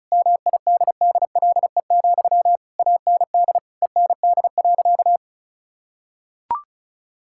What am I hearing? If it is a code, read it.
Morse code, heard as a keyed tone, about 34 wpm, 700 Hz. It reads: MIDDLE, AND END.
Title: Sentences from Top 1000 Words - Review Part 6 - 35wpm